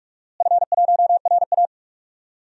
Morse code, heard as a keyed tone, about 41 wpm, 700 Hz. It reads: F1RA
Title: DXCC-100 Call Signs - 4 Characters - Repeated 2x - 45wpm